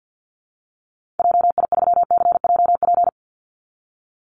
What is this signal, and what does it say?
Morse code, heard as a keyed tone, about 50 wpm, 700 Hz. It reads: JI3CJP
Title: Call Signs Repeated 3x - 50wpm